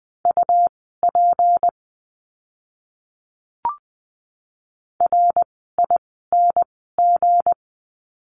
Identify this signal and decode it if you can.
Morse code, heard as a keyed tone, about 20 wpm, spaced as for 14 wpm, 700 Hz. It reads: UP RING
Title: 160 Lesson 26f - Letter g - Words 20wpm